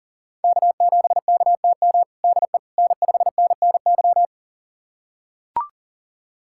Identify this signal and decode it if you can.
Morse code, heard as a keyed tone, about 40 wpm, 700 Hz. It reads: K7KTM DE N5NNY
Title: USA Callsign DE Callsign 40wpm